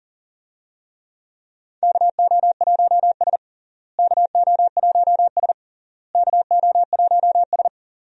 Morse code, heard as a keyed tone, about 40 wpm, 700 Hz. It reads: KO1S KO1S KO1S
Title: USA Call Signs - 4 Characters - Repeated 3x - 40wpm